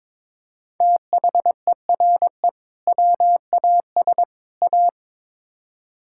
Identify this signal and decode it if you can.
Morse code, heard as a keyed tone, about 21 wpm, 700 Hz. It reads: THERE WAS A
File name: practice / 22wpm / Build A Phrase - 5-Word Phrases - 22wpm